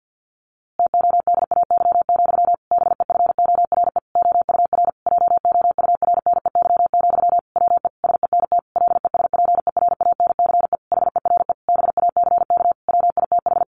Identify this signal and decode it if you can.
Morse code, heard as a keyed tone, about 50 wpm, 700 Hz. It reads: TODAY, BEFORE OUR JOURNEY, WE SENT LESPERANCE HERE BACK WITH
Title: A Sound Of Thunder 50wpm